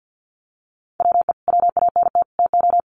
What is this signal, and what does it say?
Morse code, heard as a keyed tone, about 50 wpm, 700 Hz. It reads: WE WANT TO
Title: Sets of 3 Words 50wpm